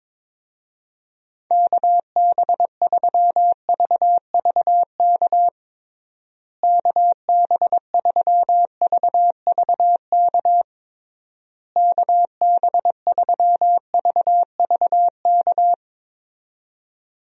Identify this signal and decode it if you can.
Morse code, heard as a keyed tone, about 22 wpm, 700 Hz. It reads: KB3VVK KB3VVK KB3VVK
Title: USA Call Signs - 6 Characters - Repeated 3x - 22wpm